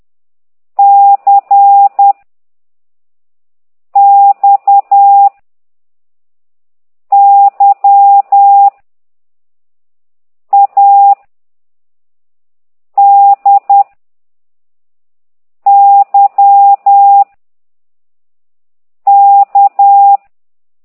Morse code, heard as a keyed tone, about 10 wpm, 800 Hz. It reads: C X Y A D Y K
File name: exerc7